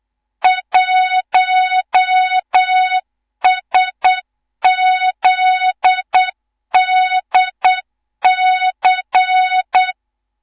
Morse code, heard as a keyed tone, about 8 wpm, 3700 Hz, with noise. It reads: 1SZDC
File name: Morze-HA2MN / hang / 18 IRASJELEK2